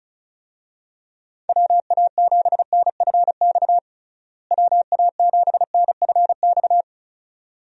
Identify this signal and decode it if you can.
Morse code, heard as a keyed tone, about 34 wpm, 700 Hz. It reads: WA7NFX WA7NFX
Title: USA Call Signs - 6 Characters - Repeated 2x - 35wpm